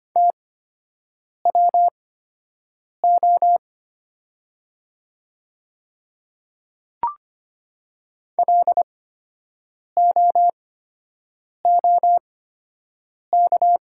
Morse code, heard as a keyed tone, about 25 wpm, spaced as for 7 wpm, 700 Hz. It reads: TWO LOOK
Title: 8x Character Spacing - Top 100 Words - 25wpm